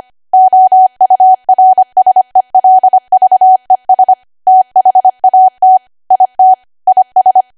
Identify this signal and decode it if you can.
Morse code, heard as a keyed tone, about 25 wpm, 750 Hz. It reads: OURSELVES THAT IT IS
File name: cwa / W1AW-Level3 / 25WPM